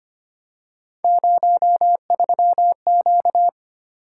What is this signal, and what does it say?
Morse code, heard as a keyed tone, about 28 wpm, 700 Hz. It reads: 03Q
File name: Speed-Racing - Three Letter-Number Combinations - 25wpm